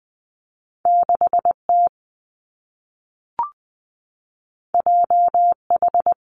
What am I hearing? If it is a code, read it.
Morse code, heard as a keyed tone, about 20 wpm, 700 Hz. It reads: 6 T J H
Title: Two Challenging Letter-Number Combinations 20wpm